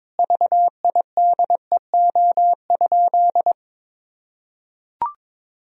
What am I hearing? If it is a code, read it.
Morse code, heard as a keyed tone, about 22 wpm, 700 Hz. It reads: VIDEO?
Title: Sentences from Top 1000 Words - Review Part 4 - 22wpm